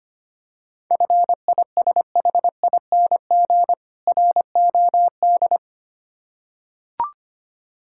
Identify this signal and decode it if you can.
Morse code, heard as a keyed tone, about 25 wpm, 700 Hz. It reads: FISHING ROD
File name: practice / 25wpm / Difficult Words with Hints - 25wpm